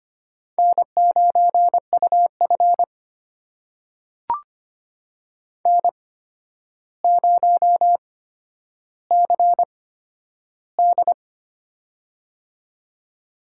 Morse code, heard as a keyed tone, about 25 wpm, spaced as for 11 wpm, 700 Hz. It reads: N9UF N0CD
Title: USA Call Signs - 4 Characters - 8x Character Spacing - 25wpm